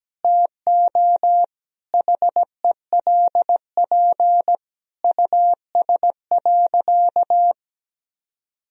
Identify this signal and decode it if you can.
Morse code, heard as a keyed tone, about 17 wpm, 700 Hz. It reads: TO HELP US.
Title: Sentences from Top 1000 Words - Review Part 2 - 17wpm